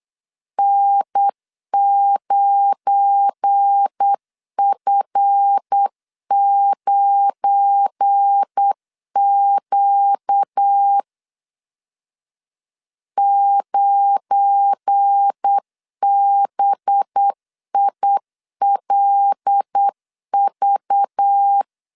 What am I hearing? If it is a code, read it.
Morse code, heard as a keyed tone, about 8 wpm, 800 Hz. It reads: N9F9Q 9BILV
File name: LEZIONI / 7 GF9Z / 7 GF9 8wpm 3545